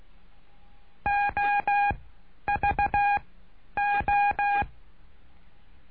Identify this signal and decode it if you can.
Morse code, heard as a keyed tone, about 16 wpm, spaced as for 10 wpm, 810 Hz. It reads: OVO
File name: cursos / CURSO DE TELEGRAFIA / Telegrafia - Trilha 09